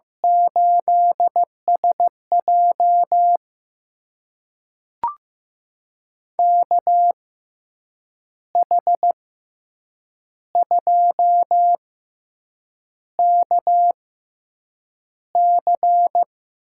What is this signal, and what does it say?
Morse code, heard as a keyed tone, about 15 wpm, spaced as for 7 wpm, 700 Hz. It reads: 8SJ KH2KC
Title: USA Call Signs - 6x Character Spacing - 15wpm